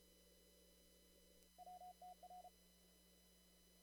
Morse code, heard as a keyed tone, about 34 wpm, 720 Hz. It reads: WTR
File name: CW / kt5j wpx cw-op-k5pi